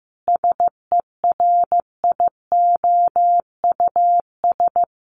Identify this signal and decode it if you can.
Morse code, heard as a keyed tone, about 15 wpm, 700 Hz. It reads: SERIOUS
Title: Sentences from Top 1500 Words 15wpm